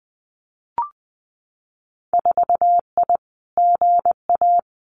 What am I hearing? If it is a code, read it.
Morse code, heard as a keyed tone, about 20 wpm, 700 Hz. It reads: 4I GA
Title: Winter Field Day Exchanges Encore 20wpm